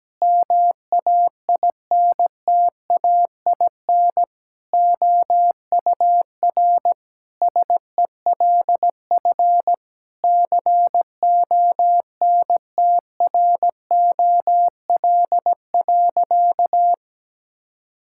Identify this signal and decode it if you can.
Morse code, heard as a keyed tone, about 17 wpm, 700 Hz. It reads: MAINTAIN OUR SELF CONTROL.
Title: Sentences from Top 1000 Words - Review Part 5 - 17wpm